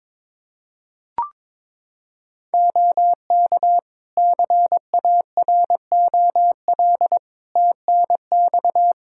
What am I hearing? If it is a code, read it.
Morse code, heard as a keyed tone, about 22 wpm, 700 Hz. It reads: OK CAROL TNX
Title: K1USN SST QSO Practice 22wpm